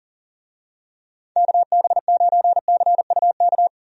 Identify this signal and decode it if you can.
Morse code, heard as a keyed tone, about 38 wpm, 700 Hz. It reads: KB9CUK
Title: USA Call Signs - 6 Characters - 40wpm